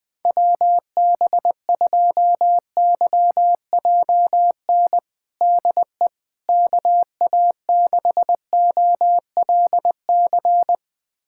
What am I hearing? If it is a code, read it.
Morse code, heard as a keyed tone, about 21 wpm, 700 Hz. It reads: WB2YJN DE KA6OLC
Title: USA Callsign DE Callsign - 6 Characters - 20wpm